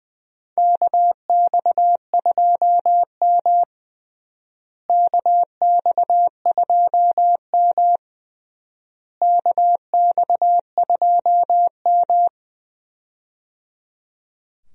Morse code, heard as a keyed tone, about 20 wpm, 700 Hz. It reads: KX2M KX2M KX2M
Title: USA Call Signs - 4 Characters - Repeated 3x - 20wpm